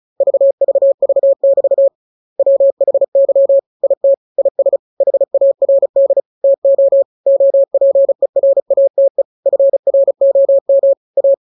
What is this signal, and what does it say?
Morse code, heard as a keyed tone, about 34 wpm, 550 Hz. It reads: VVV<BT> WHY IT IS HARD TO OPERATE FROM A